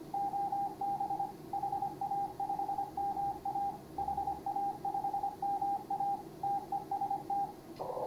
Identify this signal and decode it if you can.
Morse code, heard as a keyed tone, about 24 wpm, 810 Hz, with radio noise. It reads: OB HA5MA HA5MA TEST
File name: CW / HA5MA RHOB HANG CW 2017